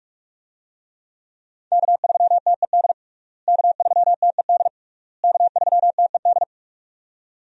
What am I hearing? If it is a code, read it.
Morse code, heard as a keyed tone, about 46 wpm, 700 Hz. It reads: K3TED K3TED K3TED
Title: USA Call Signs - 5 Characters - Repeated 3x - 45wpm